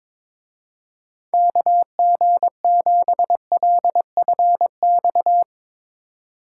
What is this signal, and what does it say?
Morse code, heard as a keyed tone, about 22 wpm, 700 Hz. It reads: KG7LFX
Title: Speed Racing - USA 2x3 Call Signs 22wpm